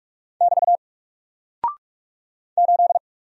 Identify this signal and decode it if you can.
Morse code, heard as a keyed tone, about 44 wpm, 700 Hz. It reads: <BT> 8
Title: Single Character (Letter:Number:Punctuation:Prosign) Alt 45wpm